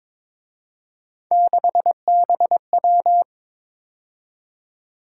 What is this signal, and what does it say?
Morse code, heard as a keyed tone, about 26 wpm, 700 Hz. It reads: 6BW
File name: Speed-Racing - Three Letter-Number Combinations - 22wpm